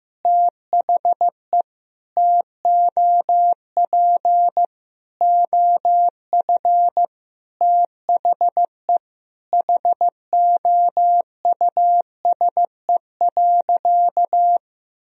Morse code, heard as a keyed tone, about 15 wpm, 700 Hz. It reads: THE TOP OF THE HOUSE.
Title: Sentences from Top 1000 Words - Review Part 2 - 15wpm